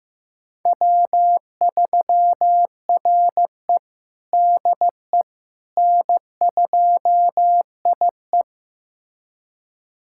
Morse code, heard as a keyed tone, about 15 wpm, 700 Hz. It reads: W3RE DE N2IE
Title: USA Callsign DE Callsign - 4 Characters - 15wpm